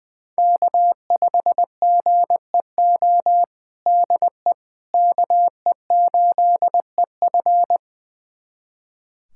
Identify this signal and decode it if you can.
Morse code, heard as a keyed tone, about 21 wpm, 700 Hz. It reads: K5GEO DE KE8EF
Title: USA Callsign DE Callsign - 5 Characters - 20wpm